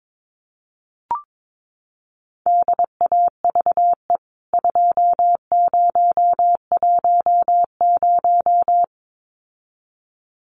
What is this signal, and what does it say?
Morse code, heard as a keyed tone, about 22 wpm, 700 Hz. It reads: DAVE 2010
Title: CWT Exchange 22wpm